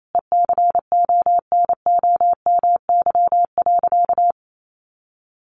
Morse code, heard as a keyed tone, about 28 wpm, 700 Hz. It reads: ECONOMY.
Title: Sentences from Top 1200 Words 28wpm